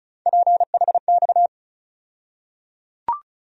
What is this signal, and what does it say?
Morse code, heard as a keyed tone, about 34 wpm, 700 Hz. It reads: PHX
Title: USA Call Sign Postfixes - 3 Letters 35wpm